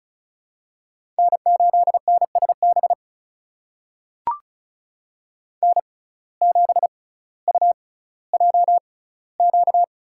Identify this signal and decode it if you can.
Morse code, heard as a keyed tone, about 36 wpm, spaced as for 18 wpm, 700 Hz. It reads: N8NSB N7UJQ
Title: USA Call Signs - 5 Characters - 6x Character Spacing - 35wpm